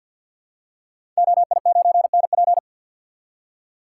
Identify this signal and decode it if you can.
Morse code, heard as a keyed tone, about 52 wpm, 700 Hz. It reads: OI9NP
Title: Code Groups - 5 Letter:Numbers - 50wpm